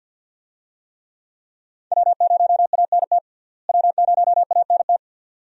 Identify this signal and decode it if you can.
Morse code, heard as a keyed tone, about 52 wpm, 700 Hz. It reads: W0ANT W0ANT
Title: USA Call Signs - 5 Characters - Repeated 2x - 50wpm